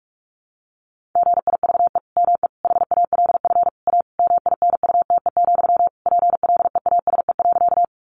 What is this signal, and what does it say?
Morse code, heard as a keyed tone, about 44 wpm, 700 Hz. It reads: GIVE ME HALF A MINUTE, PLEASE.
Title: Sentences from Top 1500 Words 45wpm